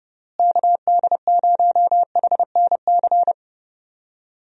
Morse code, heard as a keyed tone, about 30 wpm, 700 Hz. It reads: KD0HNC
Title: USA Call Signs - 6 Characters - Repeated 2x - 30wpm